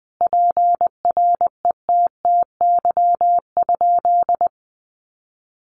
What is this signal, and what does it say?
Morse code, heard as a keyed tone, about 20 wpm, 700 Hz. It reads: PRETTY?
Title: Sentences from Top 1000 Words 20wpm